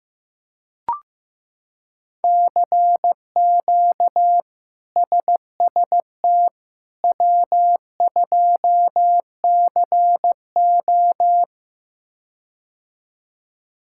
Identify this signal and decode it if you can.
Morse code, heard as a keyed tone, about 15 wpm, 700 Hz. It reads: CQ SST W2CO
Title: K1USN SST QSO Practice - 15wpm